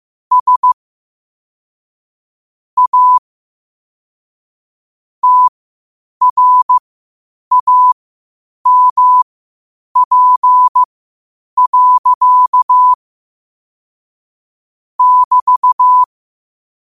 Morse code, heard as a keyed tone, about 15 wpm, spaced as for 8 wpm, 1000 Hz. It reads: S A TRAMP. <BT>
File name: qotdfiles / 8wpm